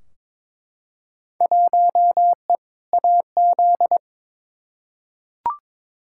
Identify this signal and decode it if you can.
Morse code, heard as a keyed tone, about 22 wpm, 700 Hz. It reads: 1E AZ
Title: Field Day Exchanges - 22wpm